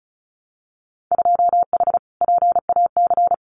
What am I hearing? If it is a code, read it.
Morse code, heard as a keyed tone, about 36 wpm, 700 Hz. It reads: 2H PAC
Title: Winter Field Day Exchanges Encore 35wpm